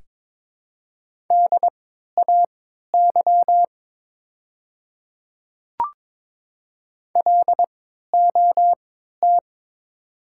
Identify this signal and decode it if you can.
Morse code, heard as a keyed tone, about 22 wpm, spaced as for 13 wpm, 690 Hz. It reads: DAY LOT
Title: 3x Character Spacing - Top 500 Words - 22wpm